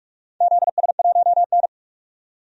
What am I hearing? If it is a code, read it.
Morse code, heard as a keyed tone, about 46 wpm, 700 Hz. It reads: ZS1N
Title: DXCC-100 Call Signs - 4 Characters - 45wpm